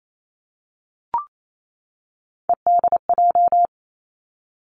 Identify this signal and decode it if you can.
Morse code, heard as a keyed tone, about 28 wpm, 700 Hz. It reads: EDJ
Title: Three Challenging Letter-Number Combinations 28wpm